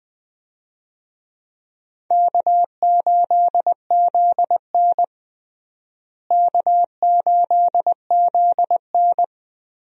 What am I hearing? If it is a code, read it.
Morse code, heard as a keyed tone, about 21 wpm, 700 Hz. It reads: K8ZN K8ZN
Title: USA Call Signs - 4 Characters - Repeated 2x - 20wpm